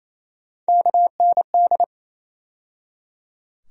Morse code, heard as a keyed tone, about 28 wpm, 700 Hz. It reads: KND
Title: USA Call Sign Postfixes - 3 Letters 28wpm